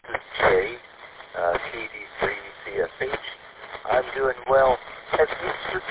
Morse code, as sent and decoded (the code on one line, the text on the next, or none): none